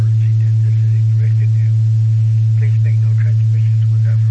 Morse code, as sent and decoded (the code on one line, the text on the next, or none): none